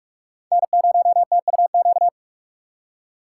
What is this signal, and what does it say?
Morse code, heard as a keyed tone, about 44 wpm, 700 Hz. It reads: N0TUQ